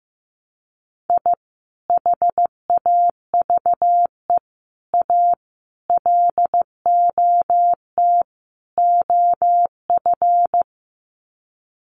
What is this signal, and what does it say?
Morse code, heard as a keyed tone, about 14 wpm, 700 Hz. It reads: I HAVE A LOT OF
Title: Sets of 5 Words 15wpm